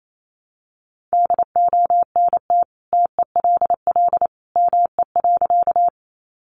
Morse code, heard as a keyed tone, about 28 wpm, 700 Hz. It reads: DONT TELL ME.
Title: Sentences from Top 300 Words 28wpm